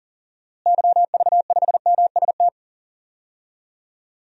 Morse code, heard as a keyed tone, about 40 wpm, 700 Hz. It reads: YV5MST
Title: DXCC-100 Call Signs - 6 Characters - Repeated 2x - 40wpm